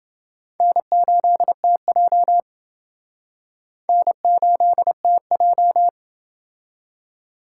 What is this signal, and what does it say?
Morse code, heard as a keyed tone, about 29 wpm, 700 Hz. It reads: N8TJ N8TJ